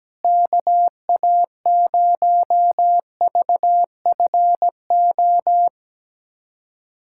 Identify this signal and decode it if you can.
Morse code, heard as a keyed tone, about 17 wpm, 700 Hz. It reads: KA0VFO
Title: USA Callsign DE Callsign - 6 Characters - 17wpm